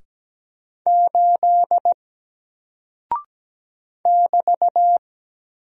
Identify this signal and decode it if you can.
Morse code, heard as a keyed tone, about 17 wpm, 700 Hz. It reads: 8 <BT>
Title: Single Character (Letter:Number:Punctuation:Prosign) Alt 17wpm